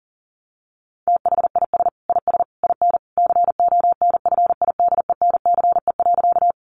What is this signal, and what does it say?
Morse code, heard as a keyed tone, about 40 wpm, 700 Hz. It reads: THIS IS IN CONFIDENCE.
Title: Sentences from Top 1500 Words 40wpm